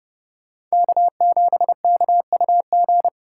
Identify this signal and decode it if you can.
Morse code, heard as a keyed tone, about 38 wpm, 700 Hz. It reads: K7KUG